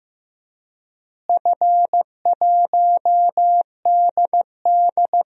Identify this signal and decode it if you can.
Morse code, heard as a keyed tone, about 16 wpm, 700 Hz. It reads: F1DD
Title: DXCC-100 Call Signs - 4 Characters - Repeated 2x - 15wpm